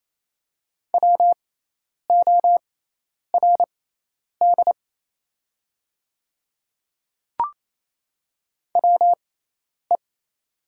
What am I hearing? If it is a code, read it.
Morse code, heard as a keyed tone, about 28 wpm, spaced as for 10 wpm, 700 Hz. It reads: WORD WE
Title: 6x Character Spacing - Top 300 Words - 28wpm